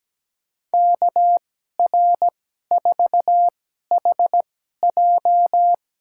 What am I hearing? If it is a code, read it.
Morse code, heard as a keyed tone, about 17 wpm, spaced as for 12 wpm, 700 Hz. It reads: KR4HJ
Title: USA Call Signs - 5 Characters - 2x Character Spacing - 17wpm